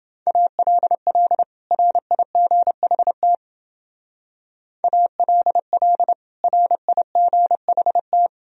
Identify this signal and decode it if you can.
Morse code, heard as a keyed tone, about 32 wpm, 700 Hz. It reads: ALL RIGHT ALL RIGHT